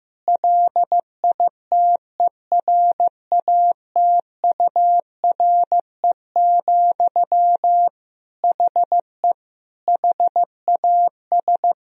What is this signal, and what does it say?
Morse code, heard as a keyed tone, about 15 wpm, 700 Hz. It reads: LITERATURE, HE HAS
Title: Sentences from Top 1000 Words - Review Part 3 - 15wpm